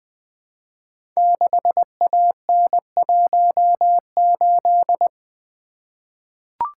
Morse code, heard as a keyed tone, about 20 wpm, 700 Hz. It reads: 6AN18
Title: Code Groups - 5 Letter:Numbers - 20wpm